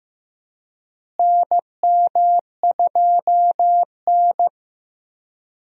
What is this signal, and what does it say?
Morse code, heard as a keyed tone, about 15 wpm, 700 Hz. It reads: NM2N
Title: USA Call Signs - 4 Characters - Repeated 3x - 15wpm